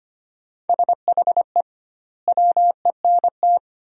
Morse code, heard as a keyed tone, about 25 wpm, 700 Hz. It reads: SHE WENT
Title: 2x Word-Spacing - Sentences from Top 200 Words - 25wpm